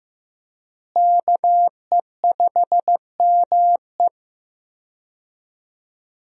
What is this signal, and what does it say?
Morse code, heard as a keyed tone, about 15 wpm, 700 Hz. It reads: KE5ME
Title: USA Call Signs - 5 Characters - Repeated 3x - 15wpm